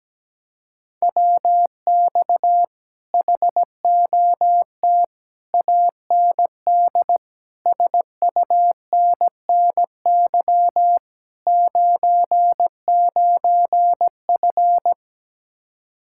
Morse code, heard as a keyed tone, about 17 wpm, 700 Hz. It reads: WX HOT AND SUNNY 99F
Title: Speed-Racing Common QSO Phrases - 17wpm